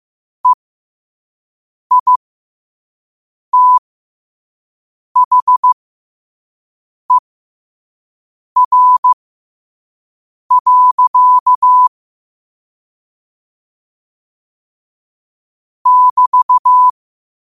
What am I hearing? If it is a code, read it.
Morse code, heard as a keyed tone, about 15 wpm, spaced as for 5 wpm, 1000 Hz. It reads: EITHER. <BT>
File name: qotdfiles / 5wpm